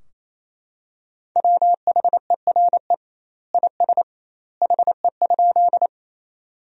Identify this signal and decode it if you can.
Morse code, heard as a keyed tone, about 28 wpm, 700 Hz. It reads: WHERE IS HE?
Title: 2x Word-Spacing - Sentences from Top 200 Words - 28wpm